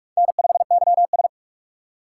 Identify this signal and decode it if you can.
Morse code, heard as a keyed tone, about 44 wpm, 700 Hz. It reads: N5YS